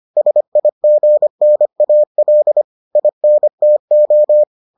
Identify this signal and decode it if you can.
Morse code, heard as a keyed tone, about 25 wpm, 600 Hz. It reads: SIGNAL INTO